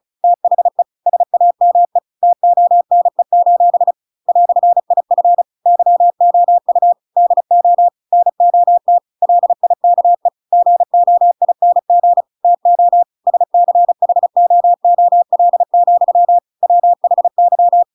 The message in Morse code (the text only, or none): THE SAME TONE: "IF YOU DO NOT LIKE GOING TO SCHOOL, WHY